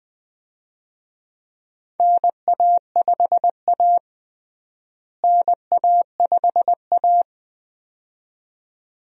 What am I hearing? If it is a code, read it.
Morse code, heard as a keyed tone, about 20 wpm, 700 Hz. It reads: NA5A NA5A
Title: USA Call Signs - 4 Characters - Repeated 2x - 20wpm